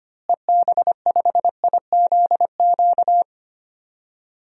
Morse code, heard as a keyed tone, about 25 wpm, 700 Hz. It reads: EB5IZQ
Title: DXCC-100 Call Signs - 6 Characters - Repeated 3x - 25wpm